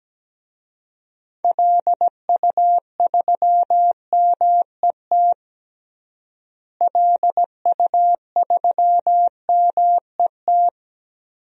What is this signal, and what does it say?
Morse code, heard as a keyed tone, about 17 wpm, 700 Hz. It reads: LU3MET LU3MET
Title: DXCC-100 Call Signs - 6 Characters - Repeated 3x - 17wpm